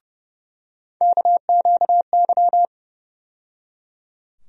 Morse code, heard as a keyed tone, about 30 wpm, 700 Hz. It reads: KQY
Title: USA Call Sign Postfixes - 3 Letters 30wpm